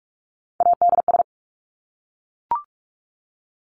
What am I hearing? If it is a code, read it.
Morse code, heard as a keyed tone, about 44 wpm, 700 Hz. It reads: ADS
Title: Abbreviations Repeated 3x - 45wpm